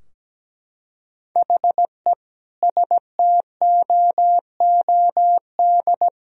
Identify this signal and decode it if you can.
Morse code, heard as a keyed tone, about 17 wpm, 700 Hz. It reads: HE STOOD
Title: Sentences from Top 1000 Words - Review Part 2 - 17wpm